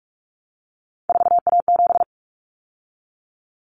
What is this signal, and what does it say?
Morse code, heard as a keyed tone, about 44 wpm, 700 Hz. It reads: 4A7
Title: Common Cut Numbers - 3-Digit 45wpm